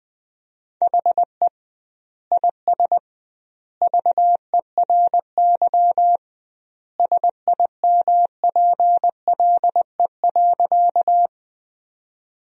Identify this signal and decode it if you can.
Morse code, heard as a keyed tone, about 20 wpm, 700 Hz. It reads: HE IS VERY SIMPLE.
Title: 2x Word-Spacing - Sentences from Top 400 Words - 20wpm